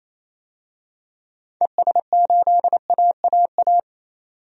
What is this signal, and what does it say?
Morse code, heard as a keyed tone, about 28 wpm, 700 Hz. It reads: ES8AAA
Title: DXCC-100 Call Signs - 6 Characters - 28wpm